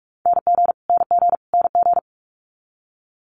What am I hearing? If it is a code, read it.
Morse code, heard as a keyed tone, about 46 wpm, 700 Hz. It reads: NG NG NG
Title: Multi-letter Phonemes Repeated 3x 45wpm